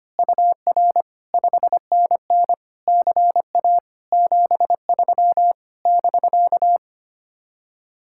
Morse code, heard as a keyed tone, about 24 wpm, 700 Hz. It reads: UR 5NN CA 73 <BK>
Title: Speed-Racing Common QSO Phrases - 25wpm